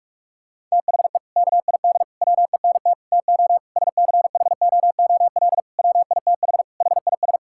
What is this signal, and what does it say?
Morse code, heard as a keyed tone, about 41 wpm, 700 Hz. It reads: THE KID WENT TO SCHOOL WITH HIS
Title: Sentences from Top 1600 Words 45wpm